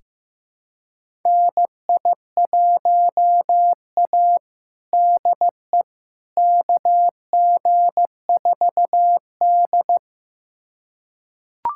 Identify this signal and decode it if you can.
Morse code, heard as a keyed tone, about 15 wpm, 700 Hz. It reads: NI1A DE KG4D
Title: USA Callsign DE Callsign - 4 Characters - 15wpm